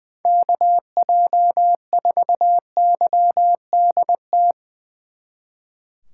Morse code, heard as a keyed tone, about 22 wpm, 700 Hz. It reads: KJ4YDT
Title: Speed Racing - USA 2x3 Call Signs 20wpm